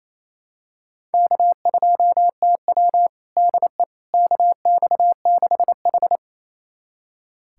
Morse code, heard as a keyed tone, about 28 wpm, 700 Hz. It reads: K2TW DE KX6H
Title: USA Callsign DE Callsign - 4 Characters - 28wpm